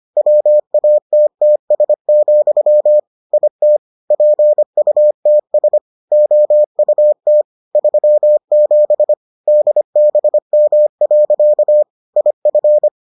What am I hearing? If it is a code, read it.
Morse code, heard as a keyed tone, about 25 wpm, 600 Hz. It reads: WATTS, IT PUTS OUT 37 DBM. IF